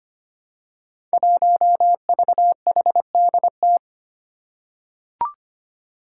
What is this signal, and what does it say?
Morse code, heard as a keyed tone, about 25 wpm, 700 Hz. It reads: 1VHDT
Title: Five Challenging Letter-Number Combinations 25wpm